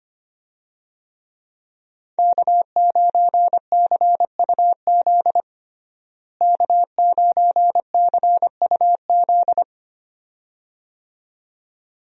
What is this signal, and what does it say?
Morse code, heard as a keyed tone, about 25 wpm, 700 Hz. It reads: K9CUZ K9CUZ
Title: USA Call Signs - 5 Characters - Repeated 2x - 25wpm